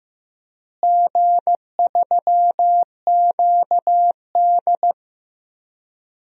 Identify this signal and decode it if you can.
Morse code, heard as a keyed tone, about 15 wpm, 700 Hz. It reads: G3QD
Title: DXCC-100 Call Signs - 4 Characters - Repeated 3x - 15wpm